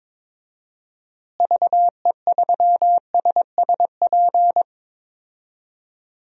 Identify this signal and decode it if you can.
Morse code, heard as a keyed tone, about 23 wpm, 700 Hz. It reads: VE3SSP
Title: DXCC-100 Call Signs - 6 Characters - Repeated 3x - 22wpm